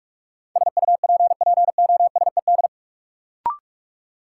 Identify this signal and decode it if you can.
Morse code, heard as a keyed tone, about 42 wpm, 700 Hz. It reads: SUPPOSED
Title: Sets of 3 Words - New Word Drill - 45wpm